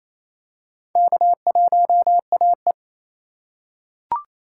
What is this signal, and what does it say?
Morse code, heard as a keyed tone, about 28 wpm, 700 Hz. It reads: K1AE
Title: K1USN SST QSO Practice - 28wpm